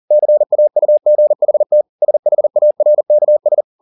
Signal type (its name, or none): Morse code